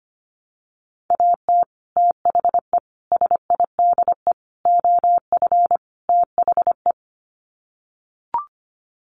Morse code, heard as a keyed tone, about 25 wpm, 700 Hz. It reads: AT THE SIDE OF THE
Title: Sets of 5 Words Encore2 25wpm